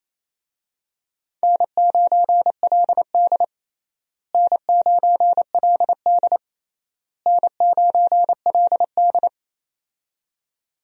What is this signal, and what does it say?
Morse code, heard as a keyed tone, about 27 wpm, 700 Hz. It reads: N9LD N9LD N9LD
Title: USA Call Signs - 4 Characters - Repeated 3x - 28wpm